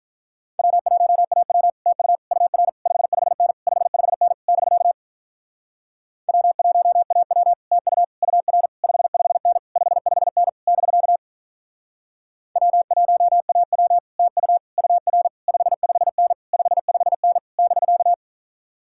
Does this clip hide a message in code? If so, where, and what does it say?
Yes: Morse code from 0.6 to 18.1 s, heard as a keyed tone, about 46 wpm, 700 Hz. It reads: W1AW TU UR 55N 55N <BK> W1AW TU UR 55N 55N <BK> W1AW TU UR 55N 55N <BK>